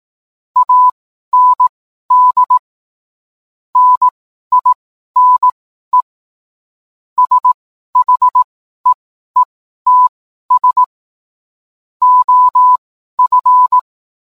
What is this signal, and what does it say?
Morse code, heard as a keyed tone, about 18 wpm, spaced as for 12 wpm, 1000 Hz. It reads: AND NINE SHEETS OF